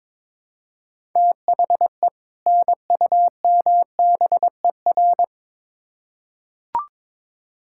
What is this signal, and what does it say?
Morse code, heard as a keyed tone, about 22 wpm, 700 Hz. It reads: THE NUMBER
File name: Build A Phrase - 2-Word Phrases - 22wpm